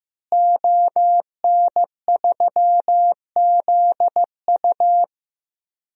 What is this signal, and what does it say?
Morse code, heard as a keyed tone, about 16 wpm, 700 Hz. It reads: ON3ZU
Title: DXCC-100 Call Signs - 5 Characters - Repeated 3x - 15wpm